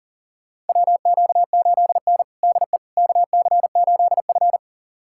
Morse code, heard as a keyed tone, about 40 wpm, 700 Hz. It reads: WQ8N DE KC8F
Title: USA Callsign DE Callsign - 4 Characters - 40wpm